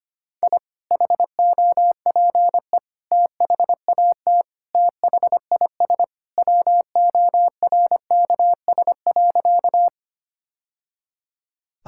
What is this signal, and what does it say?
Morse code, heard as a keyed tone, about 25 wpm, 700 Hz. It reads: I HOPE THAT THIS WORKS.
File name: practice / 25wpm / Sentences with 2-Word Phrases - 25wpm